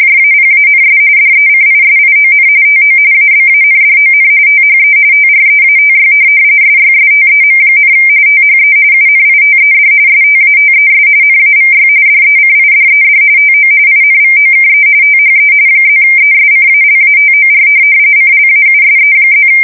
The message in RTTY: JAPANS DECISION TO ABANDON THE TREATY SYSTEM, THE UNITED STATES NAVY DECIDED TO INVOKE THE +ESCALATOR CLAUSE+

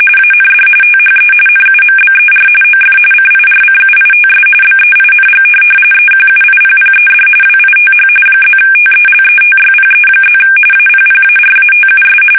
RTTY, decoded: AND THE NUMBER OF CIRCUITS BEING EQUIPPED WITH +TELETYPE+ APPARATUS

